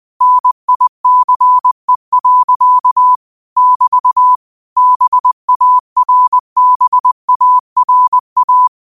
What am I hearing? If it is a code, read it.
Morse code, heard as a keyed tone, about 20 wpm, 1000 Hz. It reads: NICE. <BT> BARBARA
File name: qotdfiles / 20wpm